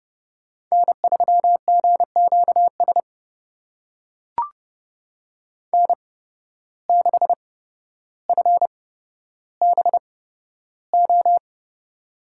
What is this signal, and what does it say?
Morse code, heard as a keyed tone, about 31 wpm, spaced as for 13 wpm, 700 Hz. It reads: N3GQS N6FBO
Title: USA Call Signs - 5 Characters - 8x Character Spacing - 30wpm